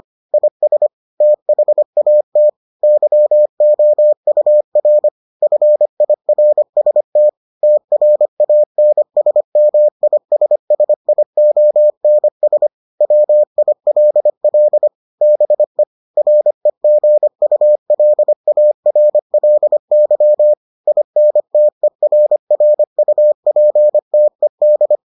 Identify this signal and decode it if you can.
Morse code, heard as a keyed tone, about 25 wpm, 600 Hz. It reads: IS THAT YOUR FIRST TRANSMISSIONS WILL BE REGULARLY INTERRUPTED